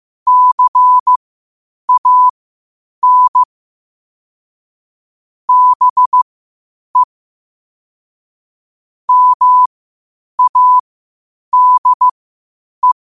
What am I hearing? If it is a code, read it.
Morse code, heard as a keyed tone, about 15 wpm, spaced as for 8 wpm, 1000 Hz. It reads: CAN BE MADE
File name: qotdfiles / 8wpm